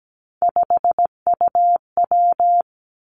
Morse code, heard as a keyed tone, about 17 wpm, 700 Hz. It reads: 5UW
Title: Three Challenging Letter-Number Combinations 17wpm